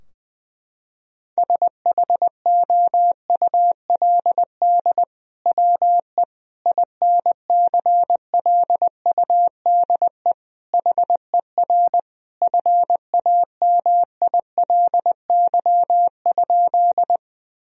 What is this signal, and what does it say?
Morse code, heard as a keyed tone, about 20 wpm, 700 Hz. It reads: SHOULD WE INCLUDE HER FAMILY?
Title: Sentences from Top 1000 Words - Review Part 3 - 20wpm